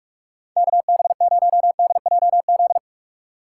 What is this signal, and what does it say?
Morse code, heard as a keyed tone, about 47 wpm, 700 Hz. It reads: KB0DJZ
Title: USA Call Signs - 2x Character Spacing - 45wpm